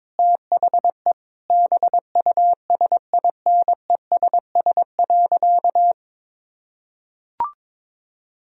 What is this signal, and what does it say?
Morse code, heard as a keyed tone, about 22 wpm, 700 Hz. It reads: THE BUSINESS.
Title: Sentences from Top 1000 Words - Review Part 3 - 22wpm